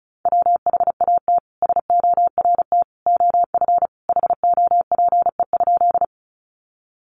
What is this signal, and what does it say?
Morse code, heard as a keyed tone, about 34 wpm, 700 Hz. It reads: WHAT SORT OF HOPE?
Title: Sentences from Top 1300 Words 35wpm